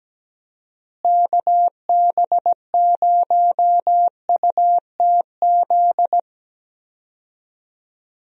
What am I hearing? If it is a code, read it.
Morse code, heard as a keyed tone, about 17 wpm, 700 Hz. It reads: KB0UTZ
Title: USA Call Signs - 6 Characters - Repeated 3x - 17wpm